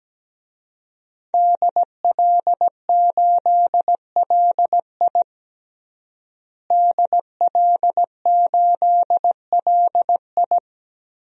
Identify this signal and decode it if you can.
Morse code, heard as a keyed tone, about 17 wpm, 700 Hz. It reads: DL8LI DL8LI
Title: DXCC-100 Call Signs - 5 Characters - Repeated 3x - 17wpm